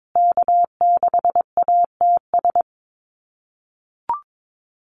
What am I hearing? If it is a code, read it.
Morse code, heard as a keyed tone, about 22 wpm, 700 Hz. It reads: K6ATS
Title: USA Call Signs Repeated 2x 22wpm